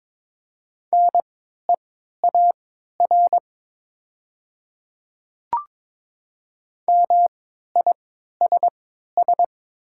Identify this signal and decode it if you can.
Morse code, heard as a keyed tone, about 22 wpm, spaced as for 12 wpm, 700 Hz. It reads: NEAR MISS
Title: 3x Character Spacing - Top 500 Words - 22wpm